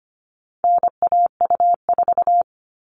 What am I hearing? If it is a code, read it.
Morse code, heard as a keyed tone, about 25 wpm, 700 Hz. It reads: NAU4
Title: All Cut Numbers - 4-Digit 25wpm